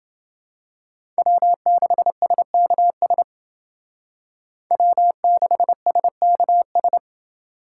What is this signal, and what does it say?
Morse code, heard as a keyed tone, about 29 wpm, 700 Hz. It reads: W6SKS W6SKS